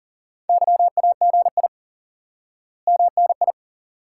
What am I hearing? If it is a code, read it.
Morse code, heard as a keyed tone, about 40 wpm, 700 Hz. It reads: YAGI MNI
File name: IQE Course 3 - Lesson 89 - YAGI - Two Elements - 40wpm